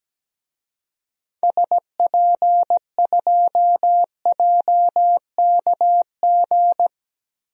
Morse code, heard as a keyed tone, about 17 wpm, 700 Hz. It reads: SP2JKG
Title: DXCC-100 Call Signs - 6 Characters - 17wpm